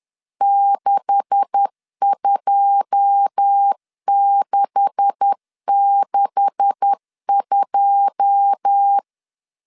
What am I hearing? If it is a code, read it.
Morse code, heard as a keyed tone, about 10 wpm, 800 Hz. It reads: 62662
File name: LEZIONI / 9 286 10wpm 2735